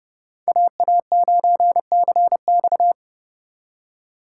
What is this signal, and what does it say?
Morse code, heard as a keyed tone, about 28 wpm, 700 Hz. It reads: AA9CX